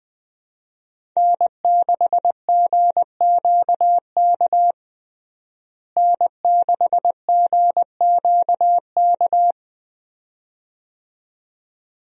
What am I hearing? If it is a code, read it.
Morse code, heard as a keyed tone, about 21 wpm, 700 Hz. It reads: N6GQK N6GQK
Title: USA Call Signs - 5 Characters - Repeated 2x - 20wpm